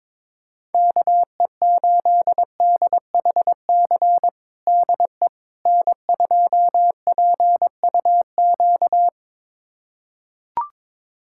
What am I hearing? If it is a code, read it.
Morse code, heard as a keyed tone, about 22 wpm, 700 Hz. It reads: KE8DHC DE N2PUQ
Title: USA Callsign DE Callsign 22wpm